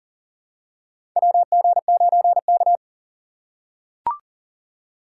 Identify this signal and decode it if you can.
Morse code, heard as a keyed tone, about 40 wpm, 700 Hz. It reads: WG9K
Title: USA Call Signs - 3x Character Spacing - 40wpm